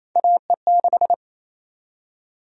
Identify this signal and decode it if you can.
Morse code, heard as a keyed tone, about 27 wpm, 700 Hz. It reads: AE6
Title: USA Call Sign Prefixes - 28wpm